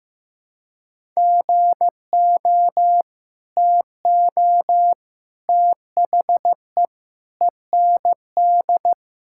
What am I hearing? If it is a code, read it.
Morse code, heard as a keyed tone, about 16 wpm, 700 Hz. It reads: GO TO THE END
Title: Sentences from Top 1000 Words - Review Part 5 - 15wpm